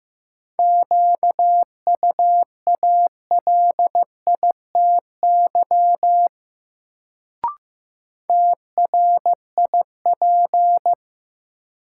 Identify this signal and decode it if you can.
Morse code, heard as a keyed tone, about 15 wpm, 700 Hz. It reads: QUALITY TRIP
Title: Sentences from Top 1000 Words - Review Part 4 - New Word Drill - 15wpm